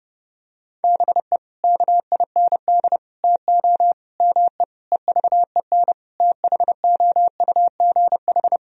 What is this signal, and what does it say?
Morse code, heard as a keyed tone, about 30 wpm, 700 Hz. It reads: BE KIND TO ME EVEN THOUGH
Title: Sentences from Top 1000 Words - Review Part 2 - 30wpm